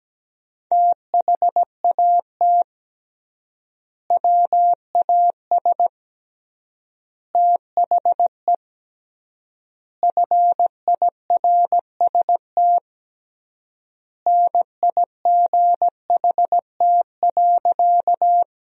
Morse code, heard as a keyed tone, about 17 wpm, 700 Hz. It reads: THAT WAS THE FIRST NIGHT.
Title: 3x Word-Spacing - Sentences from Top 300 Words - 17wpm